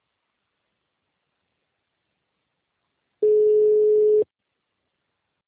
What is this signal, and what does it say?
Ringback tone (Europe, 425 Hz).